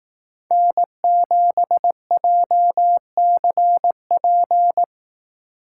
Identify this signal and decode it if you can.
Morse code, heard as a keyed tone, about 18 wpm, 700 Hz. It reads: N7JCP
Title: Speed Racing - USA 1x3 Call Signs 15wpm